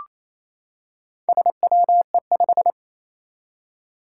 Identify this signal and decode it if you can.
Morse code, heard as a keyed tone, about 28 wpm, 700 Hz. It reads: SWE5
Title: Four Challenging Letter-Number Combinations 28wpm